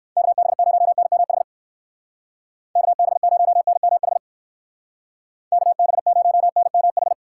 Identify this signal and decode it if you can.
Morse code, heard as a keyed tone, about 54 wpm, 700 Hz. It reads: KB0NMH KB0NMH KB0NMH
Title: Speed Racing - USA 2x3 Call Signs 45wpm